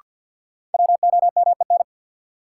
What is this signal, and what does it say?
Morse code, heard as a keyed tone, about 53 wpm, 700 Hz. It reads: WOMEN
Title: Sets of 3 Words - New Word Drill - 50wpm